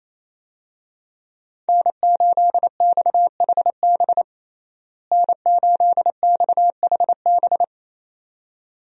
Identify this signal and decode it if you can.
Morse code, heard as a keyed tone, about 28 wpm, 700 Hz. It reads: N8XHB N8XHB
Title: USA Call Signs - 5 Characters - Repeated 2x - 28wpm